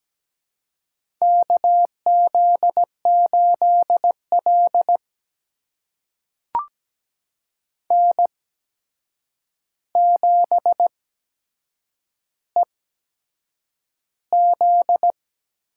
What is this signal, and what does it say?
Morse code, heard as a keyed tone, about 17 wpm, spaced as for 7 wpm, 700 Hz. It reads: KZ8L N7EZ
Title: USA Call Signs - 4 Characters - 8x Character Spacing - 17wpm